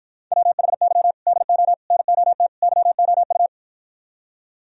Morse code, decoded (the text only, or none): WHY DO NOT YOU